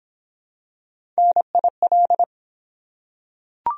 Morse code, heard as a keyed tone, about 23 wpm, 700 Hz. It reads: NIL